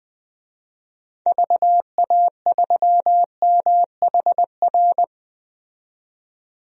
Morse code, heard as a keyed tone, about 20 wpm, 700 Hz. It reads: VA3MHR
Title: DXCC-100 Call Signs - 6 Characters - 20wpm